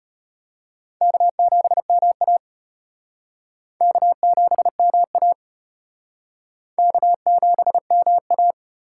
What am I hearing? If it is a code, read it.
Morse code, heard as a keyed tone, about 31 wpm, 700 Hz. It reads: K7MA K7MA K7MA
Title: Speed Racing - USA 1x2 Call Signs 25wpm